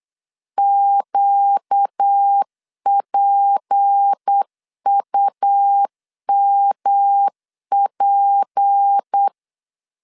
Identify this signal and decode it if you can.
Morse code, heard as a keyed tone, about 8 wpm, 800 Hz. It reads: QPUMP
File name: LEZIONI / 6 PKX0 / 6 P 8wpm 3025